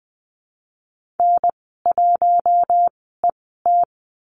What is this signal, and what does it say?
Morse code, heard as a keyed tone, about 20 wpm, spaced as for 15 wpm, 700 Hz. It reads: N1ET
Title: 069 Lesson 12g - Letter d - Call Signs 20wpm